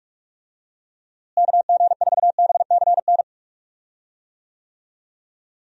Morse code, heard as a keyed tone, about 44 wpm, 700 Hz. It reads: KG4BCN
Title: USA Call Signs - 6 Characters - Repeated 3x - 45wpm